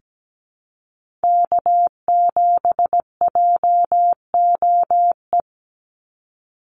Morse code, heard as a keyed tone, about 17 wpm, 700 Hz. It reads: K7JOE
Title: Call Signs - 17wpm